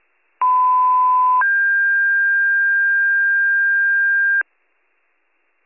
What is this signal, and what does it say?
Two-tone page, heard as a one-second tone, then a three-second tone, about 1010 then 1600 Hz, with noise. It is heard from 0.4 to 4.4 s.